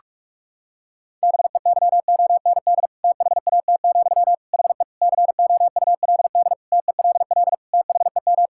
Morse code, heard as a keyed tone, about 42 wpm, 700 Hz. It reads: BEYOND THAT, HE COULD TELL THEM
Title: Sentences from Top 1000 Words - Review Part 6 - 45wpm